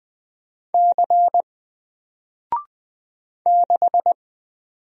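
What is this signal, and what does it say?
Morse code, heard as a keyed tone, about 20 wpm, 700 Hz. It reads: C 6
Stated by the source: Single letter and number Morse drill at 20 wpm, from N4LCD.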